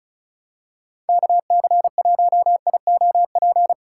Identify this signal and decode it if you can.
Morse code, heard as a keyed tone, about 36 wpm, 700 Hz. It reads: KC1IOP
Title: USA Call Signs - 6 Characters - 4x Character Spacing - 35wpm